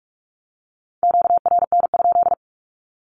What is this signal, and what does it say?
Morse code, heard as a keyed tone, about 44 wpm, 700 Hz. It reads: QRN?
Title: Q-Codes 45wpm